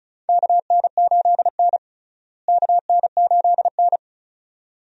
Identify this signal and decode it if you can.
Morse code, heard as a keyed tone, about 36 wpm, 700 Hz. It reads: KN8N KN8N
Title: USA Call Signs - 4 Characters - Repeated 2x - 35wpm